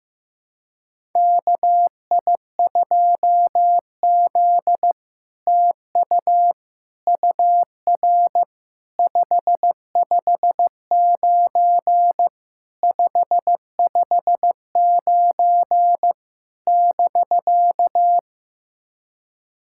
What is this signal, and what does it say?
Morse code, heard as a keyed tone, about 15 wpm, 700 Hz. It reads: KI2Z TU UR 559 559 <BK>
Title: SOTA QSO Practice 15wpm